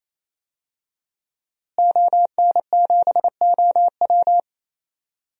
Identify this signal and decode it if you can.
Morse code, heard as a keyed tone, about 27 wpm, 700 Hz. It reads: ON7OW
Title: DXCC-100 Call Signs - 5 Characters - Repeated 3x - 28wpm